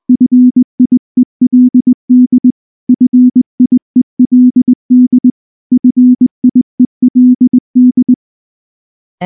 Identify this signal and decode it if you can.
Morse code, heard as a keyed tone, about 19 wpm, 250 Hz. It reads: FIELD FIELD FIELD